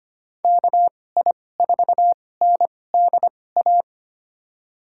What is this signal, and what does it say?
Morse code, heard as a keyed tone, about 25 wpm, spaced as for 18 wpm, 700 Hz. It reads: KI4NDA
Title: USA Call Signs - 6 Characters - 2x Character Spacing - 25wpm